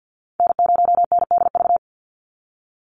Morse code, heard as a keyed tone, about 50 wpm, 700 Hz. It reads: N0NDV
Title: USA Call Signs 50wpm